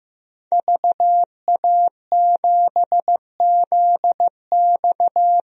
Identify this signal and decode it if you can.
Morse code, heard as a keyed tone, about 15 wpm, 700 Hz. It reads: VA7ZX
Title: DXCC-100 Call Signs - 5 Characters - Repeated 2x - 15wpm